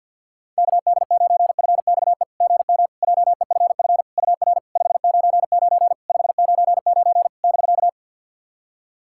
Morse code, heard as a keyed tone, about 50 wpm, 690 Hz. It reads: KD9FXE GM JEFF UR 599 599 <BK>
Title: SOTA QSO Practice 50wpm